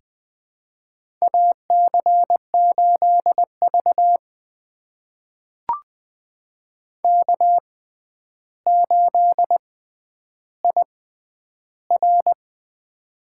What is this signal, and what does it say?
Morse code, heard as a keyed tone, about 21 wpm, spaced as for 10 wpm, 700 Hz. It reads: AC8V K8IR
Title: USA Call Signs - 4 Characters - 6x Character Spacing - 20wpm